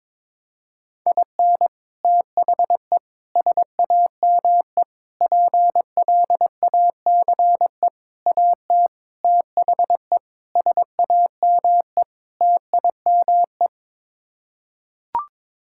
Morse code, heard as a keyed tone, about 22 wpm, 700 Hz. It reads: IN THE SAME PLACE AT THE SAME TIME
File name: practice / 22wpm / Sets of 8 Words - 22wpm